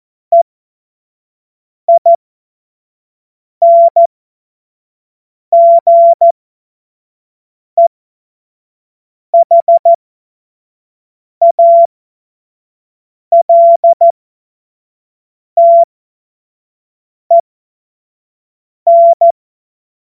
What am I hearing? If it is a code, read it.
Morse code, heard as a keyed tone, about 14 wpm, spaced as for 5 wpm, 680 Hz. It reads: EINGEHALTEN
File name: Morsekurs / Witze / Witz 03. August 2021, 25 F70